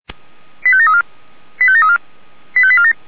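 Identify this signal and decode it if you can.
Five-tone selective call (ZVEI), heard as quick runs of five tones, tones of about 70 ms. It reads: 85734 85734 85847
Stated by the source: ZVEI five-tone selective call received by SDR, fire brigade Orlamünde (feuerwehr-orlamuende).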